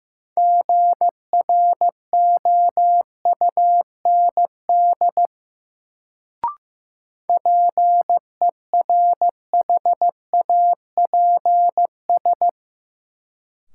Morse code, heard as a keyed tone, about 15 wpm, 700 Hz. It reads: GROUND PERHAPS
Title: Sentences from Top 500 Words - New Word Drill - 15wpm